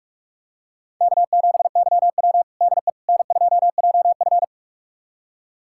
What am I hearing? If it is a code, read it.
Morse code, heard as a keyed tone, about 42 wpm, 700 Hz. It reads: K7YW DE N2JF
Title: USA Callsign DE Callsign - 4 Characters - 45wpm